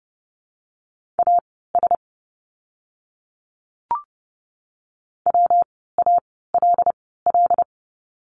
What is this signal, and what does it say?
Morse code, heard as a keyed tone, about 30 wpm, spaced as for 17 wpm, 700 Hz. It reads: AS WALL
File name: course / 30:10wpm / 115 Lesson 19f - Letter w - Words 30wpm